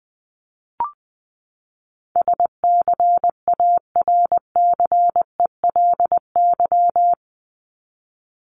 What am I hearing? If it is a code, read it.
Morse code, heard as a keyed tone, about 20 wpm, 700 Hz. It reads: SCARCELY
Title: LY Suffix - Top 100 Words with Base Words 20wpm